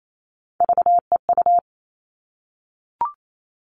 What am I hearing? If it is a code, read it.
Morse code, heard as a keyed tone, about 28 wpm, 700 Hz. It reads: VEU
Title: All Cut Numbers - 3-Digit 28wpm